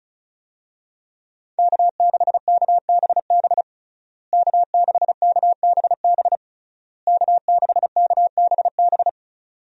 Morse code, heard as a keyed tone, about 34 wpm, 700 Hz. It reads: K6KBB K6KBB K6KBB
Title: USA Call Signs - 5 Characters - Repeated 3x - 35wpm